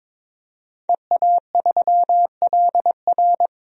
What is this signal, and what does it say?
Morse code, heard as a keyed tone, about 22 wpm, 700 Hz. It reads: EA3LR
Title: DXCC-100 Call Signs - 5 Characters - 22wpm